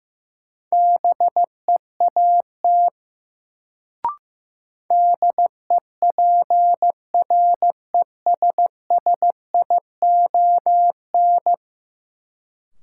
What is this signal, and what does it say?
Morse code, heard as a keyed tone, about 15 wpm, 700 Hz. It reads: BEAT DEPRESSION
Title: Sentences from Top 1300 Words - New Word Drill Alt - 15wpm